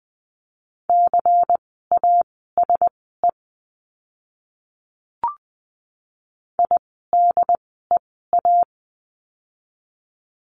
Morse code, heard as a keyed tone, about 20 wpm, spaced as for 14 wpm, 700 Hz. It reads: CASE IDEA
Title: 108 Lesson 18f - Letter m - Words 20wpm